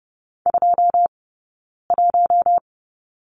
Morse code, heard as a keyed tone, about 30 wpm, spaced as for 9 wpm, 690 Hz. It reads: 21
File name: course / 30:10wpm / 086 Lesson 15d - Number 5 - Two Characters 30wpm